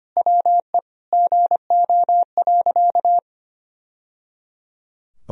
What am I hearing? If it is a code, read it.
Morse code, heard as a keyed tone, about 25 wpm, 700 Hz. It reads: WE GO.
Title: Sentences from Top 1000 Words - Review Part 2 - 25wpm